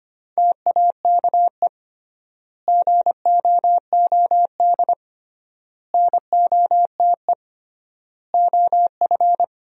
Morse code, heard as a keyed tone, about 25 wpm, 700 Hz. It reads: TAKE GOOD NOTE OF
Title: 3x Word-Spacing - Sentences from Top 500 Words - 25wpm